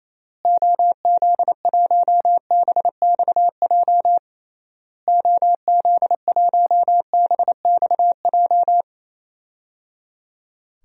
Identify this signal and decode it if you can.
Morse code, heard as a keyed tone, about 28 wpm, 700 Hz. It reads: OZ1BXJ OZ1BXJ
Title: DXCC-100 Call Signs - 6 Characters - Repeated 3x - 28wpm